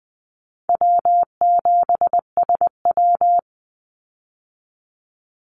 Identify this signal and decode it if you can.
Morse code, heard as a keyed tone, about 20 wpm, 700 Hz. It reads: W7SW
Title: Easy Call Signs Repeated 2x - 20wpm